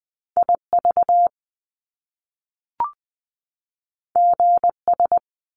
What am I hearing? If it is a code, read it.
Morse code, heard as a keyed tone, about 20 wpm, 700 Hz. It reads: IV GS
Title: Top Two Letter Combinations in Words 20wpm